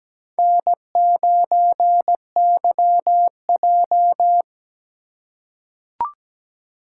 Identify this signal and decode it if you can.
Morse code, heard as a keyed tone, about 17 wpm, 700 Hz. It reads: N9YJ
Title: USA Call Signs - 4 Characters - 3x Character Spacing - 17wpm